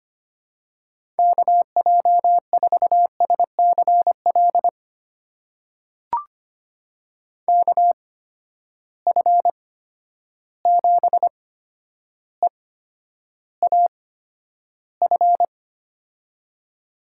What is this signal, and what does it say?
Morse code, heard as a keyed tone, about 25 wpm, spaced as for 11 wpm, 700 Hz. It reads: KJ4SCL KF7EAF
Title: USA Call Signs - 6 Characters - 8x Character Spacing - 25wpm